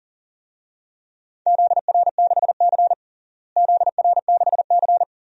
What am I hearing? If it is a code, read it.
Morse code, heard as a keyed tone, about 40 wpm, 700 Hz. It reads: ZR6C ZR6C
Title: DXCC-100 Call Signs - 4 Characters - Repeated 3x - 40wpm